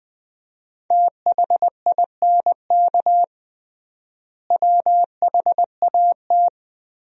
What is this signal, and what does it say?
Morse code, heard as a keyed tone, about 20 wpm, 700 Hz. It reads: THINK WHAT
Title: 3x Word-Spacing - Sentences from Top 200 Words - 20wpm